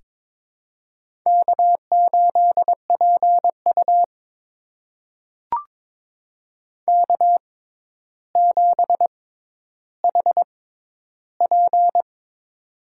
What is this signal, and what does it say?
Morse code, heard as a keyed tone, about 21 wpm, spaced as for 11 wpm, 700 Hz. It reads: K8PU K7HP
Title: USA Call Signs - 4 Characters - 6x Character Spacing - 22wpm